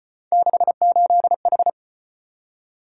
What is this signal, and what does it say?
Morse code, heard as a keyed tone, about 34 wpm, 700 Hz. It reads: 68H